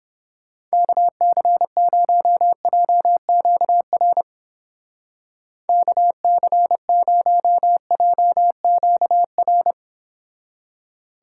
Morse code, heard as a keyed tone, about 31 wpm, 700 Hz. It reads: KC0JQR KC0JQR